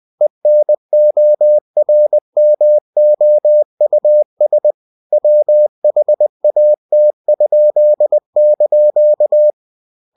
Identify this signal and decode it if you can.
Morse code, heard as a keyed tone, about 20 wpm, 600 Hz. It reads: ENORMOUS WHAT?)
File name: cw-books / winnie-the-pooh / winnie-20wpm-page-0104